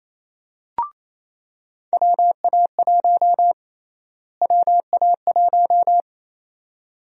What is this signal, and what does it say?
Morse code, heard as a keyed tone, about 28 wpm, 700 Hz. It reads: WA1 WA1
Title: USA Call Sign Prefixes - Repeated 2x - 28wpm